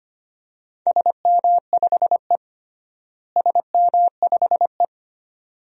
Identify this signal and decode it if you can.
Morse code, heard as a keyed tone, about 25 wpm, 700 Hz. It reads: SM5E SM5E
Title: DXCC-100 Call Signs - 4 Characters - Repeated 2x - 25wpm